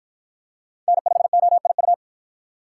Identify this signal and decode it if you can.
Morse code, heard as a keyed tone, about 53 wpm, 700 Hz. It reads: N5OIU